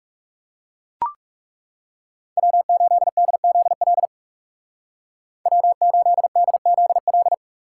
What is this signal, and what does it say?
Morse code, heard as a keyed tone, about 36 wpm, 700 Hz. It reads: W8DZL W8DZL
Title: Speed Racing - USA 1x3 Call Signs 30wpm